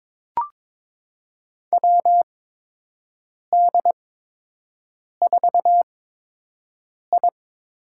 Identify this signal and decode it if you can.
Morse code, heard as a keyed tone, about 22 wpm, spaced as for 6 wpm, 700 Hz. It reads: WD4I
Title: USA Call Signs - 8x Character Spacing - 22wpm